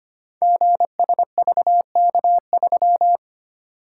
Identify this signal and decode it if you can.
Morse code, heard as a keyed tone, about 25 wpm, 700 Hz. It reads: GSVK3
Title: Code Groups - 5 Letter:Numbers - 25wpm